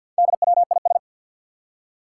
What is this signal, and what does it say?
Morse code, heard as a keyed tone, about 48 wpm, 700 Hz. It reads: DWIS
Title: Four Challenging Letter-Number Combinations 50wpm